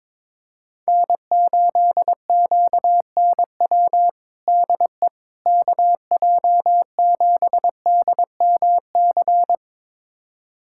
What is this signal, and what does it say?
Morse code, heard as a keyed tone, about 22 wpm, 700 Hz. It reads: N8QNW DE KJ7DMC
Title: USA Callsign DE Callsign 22wpm